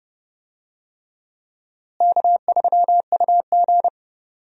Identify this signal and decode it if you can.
Morse code, heard as a keyed tone, about 32 wpm, 700 Hz. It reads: K3UG
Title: USA Call Signs - 4 Characters - Repeated 3x - 30wpm